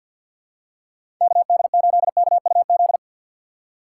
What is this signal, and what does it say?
Morse code, heard as a keyed tone, about 50 wpm, 700 Hz. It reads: KD8KUZ
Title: USA Call Signs - 6 Characters - 4x Character Spacing - 50wpm